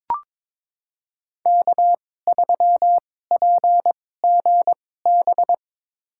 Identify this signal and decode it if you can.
Morse code, heard as a keyed tone, about 22 wpm, spaced as for 16 wpm, 690 Hz. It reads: K3PGB
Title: USA Call Signs - 5 Characters - 2x Character Spacing - 22wpm